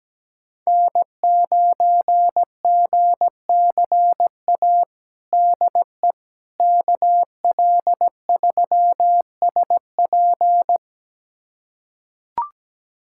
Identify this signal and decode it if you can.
Morse code, heard as a keyed tone, about 17 wpm, 700 Hz. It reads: N9GCA DE KL3SP
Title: USA Callsign DE Callsign - 5 Characters - 17wpm